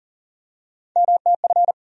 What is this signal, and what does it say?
Morse code, heard as a keyed tone, about 40 wpm, 700 Hz. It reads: MTF
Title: USA Call Sign Postfixes - 3 Letters 40wpm